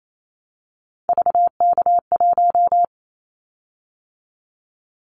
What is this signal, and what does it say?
Morse code, heard as a keyed tone, about 28 wpm, 700 Hz. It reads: VK1
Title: Callsign Prefixes for Top 100 DXCC - 28wpm